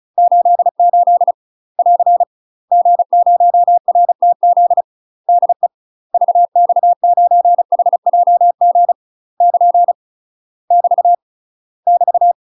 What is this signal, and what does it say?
Morse code, heard as a keyed tone, about 34 wpm, 700 Hz. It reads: 88 <AR> G0RTZ DE VX9HJG <KN> <BT> <BT>
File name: UK MASTER QSOs TRAINING / 35WPM / UK MASTER QSOs 0003